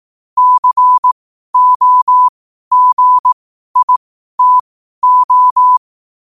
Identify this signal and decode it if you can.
Morse code, heard as a keyed tone, about 18 wpm, spaced as for 12 wpm, 1000 Hz. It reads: COGITO
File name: qotdfiles / 12wpm